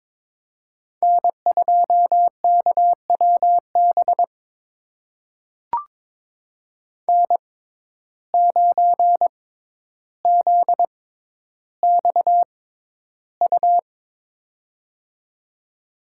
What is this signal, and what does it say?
Morse code, heard as a keyed tone, about 22 wpm, spaced as for 11 wpm, 700 Hz. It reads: N2KWB N9ZXU
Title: USA Call Signs - 5 Characters - 6x Character Spacing - 22wpm